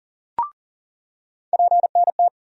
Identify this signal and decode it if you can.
Morse code, heard as a keyed tone, about 40 wpm, 700 Hz. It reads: PNT
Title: USA Call Sign Postfixes - 3 Letters 40wpm